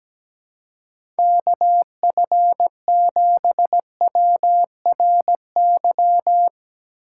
Morse code, heard as a keyed tone, about 17 wpm, 700 Hz. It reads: KF7WRY
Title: USA Call Signs - 6 Characters - 2x Character Spacing - 17wpm